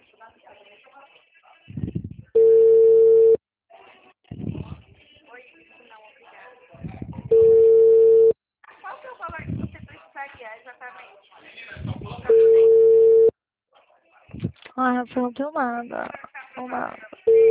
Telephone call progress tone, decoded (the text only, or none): ringback tone